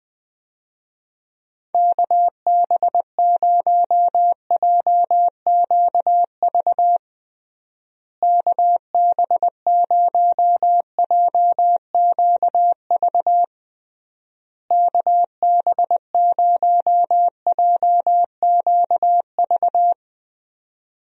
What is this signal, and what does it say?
Morse code, heard as a keyed tone, about 21 wpm, 700 Hz. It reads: KB0JQV KB0JQV KB0JQV
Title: USA Call Signs - 6 Characters - Repeated 3x - 20wpm